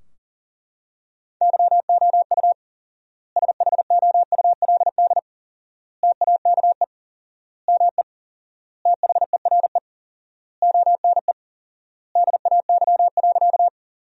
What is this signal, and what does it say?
Morse code, heard as a keyed tone, about 40 wpm, 700 Hz. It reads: YOU SHOULD TAKE ME THERE ONE DAY.
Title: 4x Word-Spacing - Sentences from Top 300 Words - 40wpm